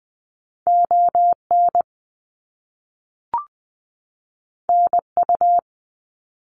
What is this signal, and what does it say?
Morse code, heard as a keyed tone, about 20 wpm, 700 Hz. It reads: ON NU
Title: Canadian Provinces and Territories 20wpm